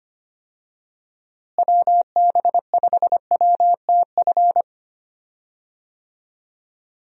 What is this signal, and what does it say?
Morse code, heard as a keyed tone, about 25 wpm, 700 Hz. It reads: WB5WTF
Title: USA Call Signs - 6 Characters - Repeated 2x - 25wpm